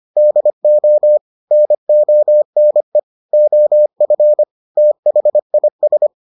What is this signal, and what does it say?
Morse code, heard as a keyed tone, about 25 wpm, 600 Hz. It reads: DO NONE OF THIS